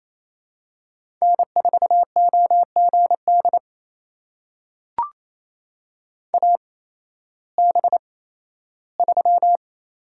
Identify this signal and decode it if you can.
Morse code, heard as a keyed tone, about 28 wpm, spaced as for 15 wpm, 700 Hz. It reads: N4OGD AB3
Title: USA Call Signs - 8x Character Spacing - 28wpm